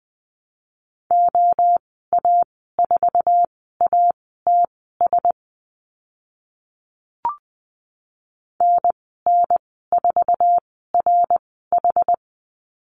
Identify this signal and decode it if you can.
Morse code, heard as a keyed tone, about 20 wpm, spaced as for 14 wpm, 700 Hz. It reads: OA4ATS NN4RH
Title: 076 Lesson 13g - Letter L - Call Signs 20wpm